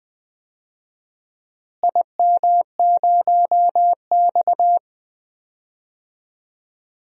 Morse code, heard as a keyed tone, about 21 wpm, 700 Hz. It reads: IM0X
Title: DXCC-100 Call Signs - 4 Characters - Repeated 2x - 20wpm